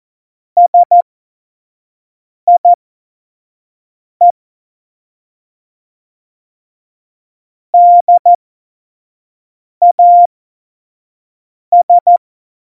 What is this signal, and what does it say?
Morse code, heard as a keyed tone, about 14 wpm, spaced as for 5 wpm, 700 Hz. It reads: SIE DAS
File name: Morsekurs / Witze / Witz 13. August 2019, 25 F70